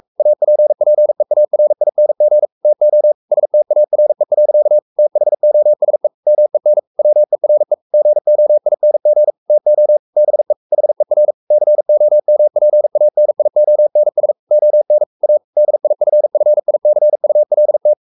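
Morse code, heard as a keyed tone, about 44 wpm, 600 Hz. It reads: APPEARING TO STARE. THOSE MEN WERE GOING TO BE HER COMPANIONS ON A DIFFICULT
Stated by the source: Morse code at 43 wpm sending Ken Follett's novel Never in English, from CW med Gnister.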